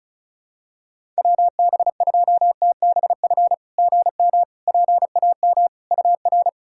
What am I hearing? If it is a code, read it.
Morse code, heard as a keyed tone, about 36 wpm, 690 Hz. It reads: WB2TBF GM PAM UR
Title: SOTA QSO Practice 35wpm